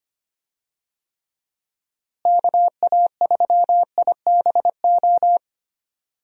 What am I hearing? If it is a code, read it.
Morse code, heard as a keyed tone, about 25 wpm, 700 Hz. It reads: KA3IBO
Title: USA Call Signs - 6 Characters - Repeated 2x - 25wpm